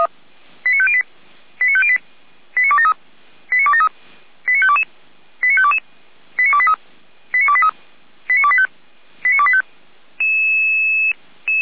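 Five-tone selective call (ZVEI), heard as quick runs of five tones, tones of about 70 ms. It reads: 89508 89508 89273 89273 89522 89522 89283 89283 89285 89285
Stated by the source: ZVEI five-tone selective call 89099 received via SDR, from the feuerwehr-orlamuende fire brigade audio server.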